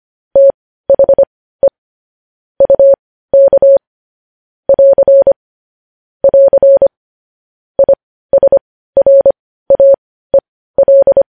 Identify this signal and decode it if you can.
Morse code, heard as a keyed tone, about 25 wpm, spaced as for 15 wpm, 550 Hz. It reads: THE UK <AR> <AR> ISRAEL
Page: news headlines in morse code at 15 wpm